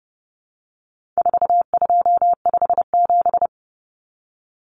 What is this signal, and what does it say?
Morse code, heard as a keyed tone, about 30 wpm, 700 Hz. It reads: 4257
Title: Numbers - 4-Digit 30wpm